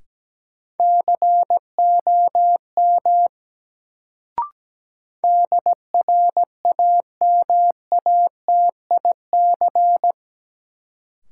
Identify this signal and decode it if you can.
Morse code, heard as a keyed tone, about 17 wpm, 700 Hz. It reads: COM DRAMATIC
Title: Sentences from Top 1300 Words - New Word Drill Alt - 17wpm